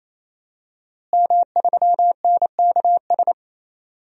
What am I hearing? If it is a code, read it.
Morse code, heard as a keyed tone, about 28 wpm, 700 Hz. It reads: M3NKS
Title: DXCC-100 Call Signs - 5 Characters - Repeated 3x - 28wpm